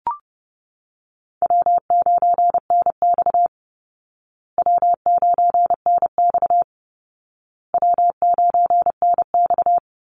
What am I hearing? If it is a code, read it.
Morse code, heard as a keyed tone, about 30 wpm, 700 Hz. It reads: W9NX W9NX W9NX
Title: Easy Call Signs Repeated 3x - 30wpm